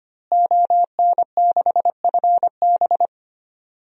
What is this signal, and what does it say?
Morse code, heard as a keyed tone, about 25 wpm, 700 Hz. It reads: ON6FB
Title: DXCC-100 Call Signs - 5 Characters - Repeated 3x - 25wpm